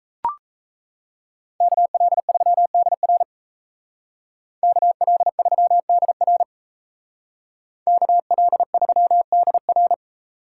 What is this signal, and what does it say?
Morse code, heard as a keyed tone, about 36 wpm, 700 Hz. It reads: KL3DR KL3DR KL3DR